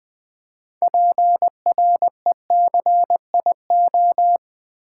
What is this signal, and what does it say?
Morse code, heard as a keyed tone, about 20 wpm, 700 Hz. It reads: PRECIO
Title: Build A Word - 2 Letters At A Time - 8-Letter Words - 20wpm